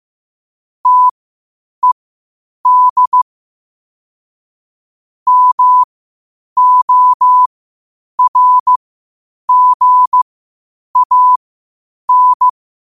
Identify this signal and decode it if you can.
Morse code, heard as a keyed tone, about 15 wpm, spaced as for 8 wpm, 1000 Hz. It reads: TED MORGAN
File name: qotdfiles / 8wpm